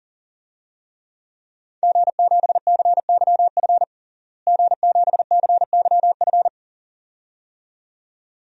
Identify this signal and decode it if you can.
Morse code, heard as a keyed tone, about 40 wpm, 700 Hz. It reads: G7CYF G7CYF
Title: DXCC-100 Call Signs - 5 Characters - Repeated 2x - 40wpm